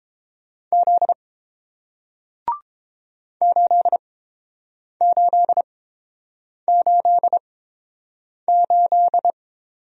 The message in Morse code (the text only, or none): Z 8888